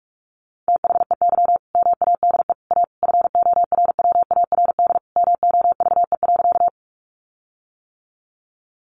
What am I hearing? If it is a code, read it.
Morse code, heard as a keyed tone, about 46 wpm, 700 Hz. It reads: THEY MADE A FORWARD MOVE.
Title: Sentences from Top 1300 Words 45wpm